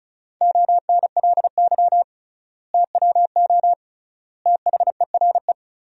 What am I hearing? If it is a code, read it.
Morse code, heard as a keyed tone, about 36 wpm, 700 Hz. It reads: ONLY TWO THERE
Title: 3x Word-Spacing - Sentences from Top 200 Words - 35wpm